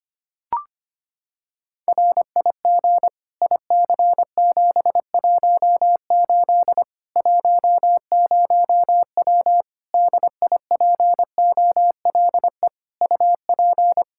Morse code, heard as a keyed tone, about 25 wpm, 700 Hz. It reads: RIG IC718 10W DIPOLE UP
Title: Practice QSO - Exchange 2 - Info - 25wpm